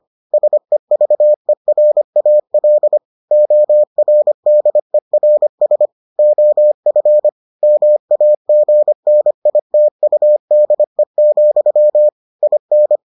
Morse code, heard as a keyed tone, about 25 wpm, 600 Hz. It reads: SEVERAL ORDERS OF MAGNITUDE, IN